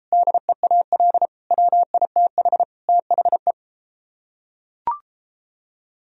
Morse code, heard as a keyed tone, about 32 wpm, 700 Hz. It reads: DEAL WITH THE